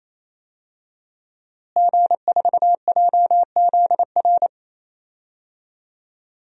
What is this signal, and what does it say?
Morse code, heard as a keyed tone, about 28 wpm, 700 Hz. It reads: G4JZR
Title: DXCC-100 Call Signs - 5 Characters - Repeated 3x - 28wpm